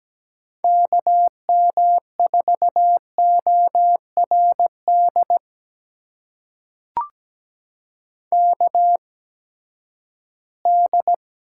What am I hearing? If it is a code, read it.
Morse code, heard as a keyed tone, about 17 wpm, spaced as for 12 wpm, 700 Hz. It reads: KM4ORD KD
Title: USA Call Signs - 6 Characters - 8x Character Spacing - 17wpm